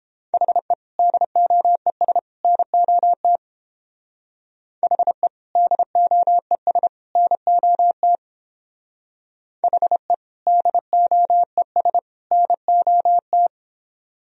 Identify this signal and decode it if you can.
Morse code, heard as a keyed tone, about 30 wpm, 700 Hz. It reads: HE DOES NOT HE DOES NOT HE DOES NOT